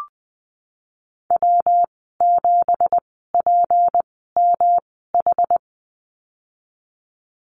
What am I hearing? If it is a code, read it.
Morse code, heard as a keyed tone, about 20 wpm, spaced as for 15 wpm, 700 Hz. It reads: W7PMH
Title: 207 Lesson 33g - Letter j - Call Signs 20wpm